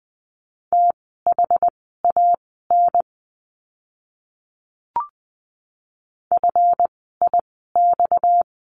Morse code, heard as a keyed tone, about 20 wpm, spaced as for 14 wpm, 700 Hz. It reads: THAN FIX
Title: 230 Lesson 37f - Letter x - Words 20wpm